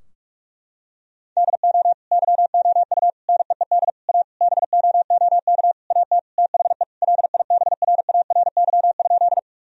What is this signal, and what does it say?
Morse code, heard as a keyed tone, about 46 wpm, 700 Hz. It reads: DO YOU NEED A BOOK AT THE LIBRARY?
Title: Sentences from Top 1000 Words - Review Part 4 - 45wpm